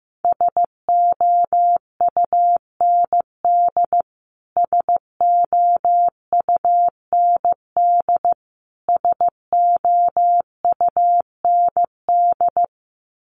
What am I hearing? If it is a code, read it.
Morse code, heard as a keyed tone, about 15 wpm, 700 Hz. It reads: SOUND SOUND SOUND
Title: Top 500 Words Repeated 15wpm